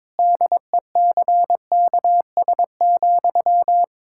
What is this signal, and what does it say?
Morse code, heard as a keyed tone, about 22 wpm, 700 Hz. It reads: DECKS,
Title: Pangram Challenge v2 - 22wpm